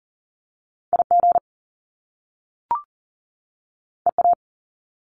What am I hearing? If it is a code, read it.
Morse code, heard as a keyed tone, about 41 wpm, 700 Hz. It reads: IG EA
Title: Top Two Letter Combinations in Words 40wpm